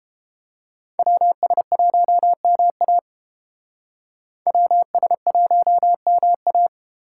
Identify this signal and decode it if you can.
Morse code, heard as a keyed tone, about 30 wpm, 700 Hz. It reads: WS1MA WS1MA